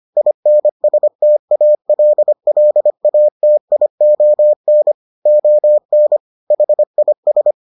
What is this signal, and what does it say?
Morse code, heard as a keyed tone, about 25 wpm, 600 Hz. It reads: INSTALLATION ON HIS